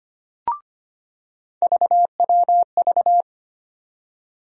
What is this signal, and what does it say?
Morse code, heard as a keyed tone, about 25 wpm, 700 Hz. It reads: VWV
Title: USA Call Sign Postfixes - 3 Letters 25wpm